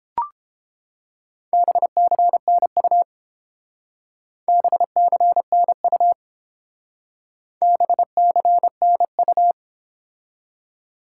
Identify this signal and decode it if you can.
Morse code, heard as a keyed tone, about 30 wpm, 700 Hz. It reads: BCNU BCNU BCNU